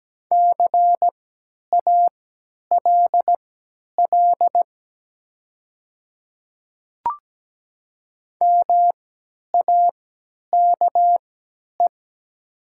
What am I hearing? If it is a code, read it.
Morse code, heard as a keyed tone, about 17 wpm, spaced as for 10 wpm, 700 Hz. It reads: CALL MAKE
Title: 3x Character Spacing - Top 300 Words - 17wpm